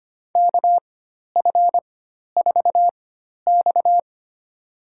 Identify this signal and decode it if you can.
Morse code, heard as a keyed tone, about 25 wpm, spaced as for 12 wpm, 700 Hz. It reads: KF4X
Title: USA Call Signs - 4 Characters - 4x Character Spacing - 25wpm